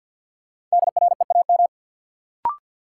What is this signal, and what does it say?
Morse code, heard as a keyed tone, about 53 wpm, 700 Hz. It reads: DREAM
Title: Sentences from Top 1500 Words - New Word Drill Alt - 50wpm